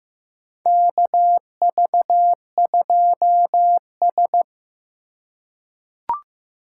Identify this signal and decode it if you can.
Morse code, heard as a keyed tone, about 15 wpm, 700 Hz. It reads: KV2S
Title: USA Call Signs - 4 Characters - 4x Character Spacing - 15wpm